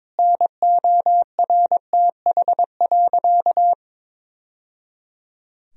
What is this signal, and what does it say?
Morse code, heard as a keyed tone, about 21 wpm, 700 Hz. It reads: NORTH.
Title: Sentences from Top 1000 Words - Review Part 1 - 22wpm